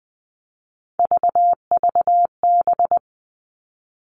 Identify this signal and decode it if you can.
Morse code, heard as a keyed tone, about 20 wpm, 700 Hz. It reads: VVB
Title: Three Challenging Letter-Number Combinations 20wpm